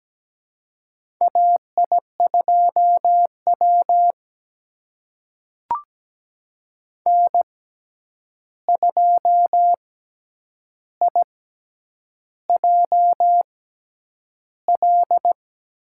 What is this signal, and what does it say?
Morse code, heard as a keyed tone, about 17 wpm, spaced as for 8 wpm, 700 Hz. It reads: AI2W N2IJL
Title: USA Call Signs - 6x Character Spacing - 17wpm